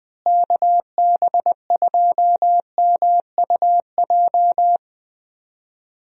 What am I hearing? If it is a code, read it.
Morse code, heard as a keyed tone, about 20 wpm, 700 Hz. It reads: KB2MUJ
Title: USA Call Signs - 6 Characters - Repeated 3x - 20wpm